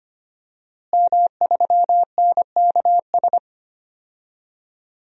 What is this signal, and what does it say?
Morse code, heard as a keyed tone, about 25 wpm, 700 Hz. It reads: M3NKS
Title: DXCC-100 Call Signs - 5 Characters - Repeated 3x - 25wpm